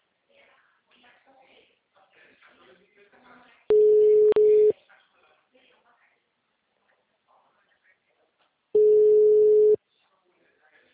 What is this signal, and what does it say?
Ringback tone (Europe, 425 Hz), with speech.